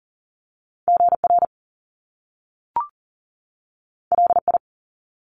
Morse code, heard as a keyed tone, about 39 wpm, 700 Hz. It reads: GR LI
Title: Top Two Letter Combinations in Words 40wpm